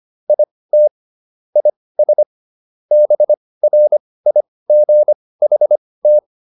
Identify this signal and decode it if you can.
Morse code, heard as a keyed tone, about 25 wpm, spaced as for 18 wpm, 600 Hz. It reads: IT IS BRIGHT